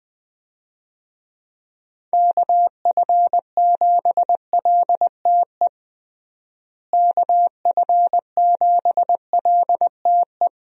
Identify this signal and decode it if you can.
Morse code, heard as a keyed tone, about 21 wpm, 700 Hz. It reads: KF7LTE KF7LTE
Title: USA Call Signs - 6 Characters - Repeated 2x - 20wpm